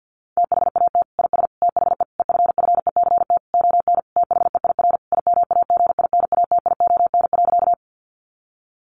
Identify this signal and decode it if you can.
Morse code, heard as a keyed tone, about 50 wpm, 700 Hz. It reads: THAT IS THE EFFECT ON THEIR IMAGINATION.
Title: Sentences from Top 1300 Words 50wpm